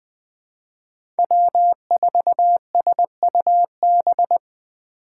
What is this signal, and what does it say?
Morse code, heard as a keyed tone, about 20 wpm, 700 Hz. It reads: W4SUB
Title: USA Call Signs - 4x Character Spacing - 20wpm